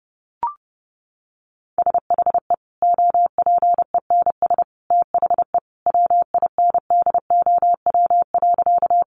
Morse code, heard as a keyed tone, about 30 wpm, 700 Hz. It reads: SHE OPENS THE WINDOW.
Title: Easy Phrases 30wpm